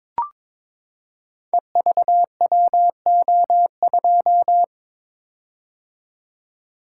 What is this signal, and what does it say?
Morse code, heard as a keyed tone, about 22 wpm, 700 Hz. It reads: EVWO2
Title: Code Groups - 5 Letter:Numbers - 22wpm